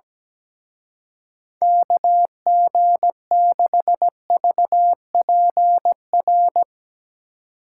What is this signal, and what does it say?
Morse code, heard as a keyed tone, about 18 wpm, 700 Hz. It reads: KG6VPR
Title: Speed Racing - USA 2x3 Call Signs 17wpm